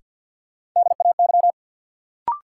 Morse code, heard as a keyed tone, about 53 wpm, 700 Hz. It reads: DAY
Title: Sentences from Top 200 Words - New Word Drill - 50wpm